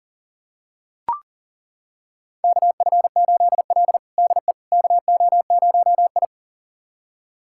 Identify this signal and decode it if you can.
Morse code, heard as a keyed tone, about 40 wpm, 700 Hz. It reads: KF8L DE KO0I
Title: USA Callsign DE Callsign - 4 Characters - 40wpm